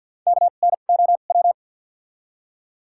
Morse code, heard as a keyed tone, about 48 wpm, spaced as for 35 wpm, 700 Hz. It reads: KNOW
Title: 2x Character Spacing - Top 500 Words - 50wpm